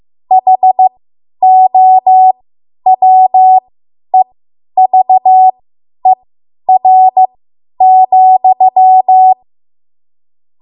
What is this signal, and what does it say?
Morse code, heard as a keyed tone, about 15 wpm, spaced as for 10 wpm, 750 Hz. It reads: HOWEVER,
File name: Archive / 10 WPM / 10WPM